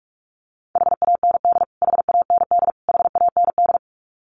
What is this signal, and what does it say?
Morse code, heard as a keyed tone, about 44 wpm, 700 Hz. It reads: HAND HAND HAND
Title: Top 200 Words Repeated 45wpm